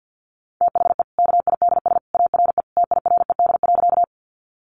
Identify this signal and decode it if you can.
Morse code, heard as a keyed tone, about 50 wpm, 700 Hz. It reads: THE KIDS ARE TIRED.
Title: Easy Phrases 50wpm